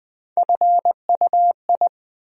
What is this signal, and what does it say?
Morse code, heard as a keyed tone, about 20 wpm, 700 Hz. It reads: FUI
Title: USA Call Sign Postfixes - 3 Letters 20wpm